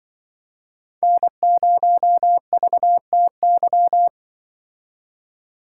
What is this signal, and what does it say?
Morse code, heard as a keyed tone, about 24 wpm, 700 Hz. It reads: N0VTY
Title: Speed Racing - USA Call Signs 20wpm